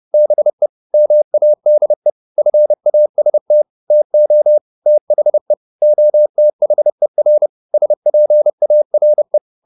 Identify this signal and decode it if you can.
Morse code, heard as a keyed tone, about 30 wpm, 600 Hz. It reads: BE MADE FAST TO THE OTHER SPARE
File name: the old man and the sea / the old man and the sea 1 30wpm